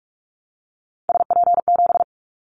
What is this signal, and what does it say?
Morse code, heard as a keyed tone, about 44 wpm, 700 Hz. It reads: SP7
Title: Callsign Prefixes for Top 25 DXCC - Repeated 3x - 45wpm